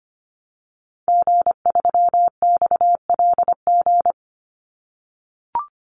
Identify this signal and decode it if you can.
Morse code, heard as a keyed tone, about 25 wpm, 700 Hz. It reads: G3XLG
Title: Call Signs - 25wpm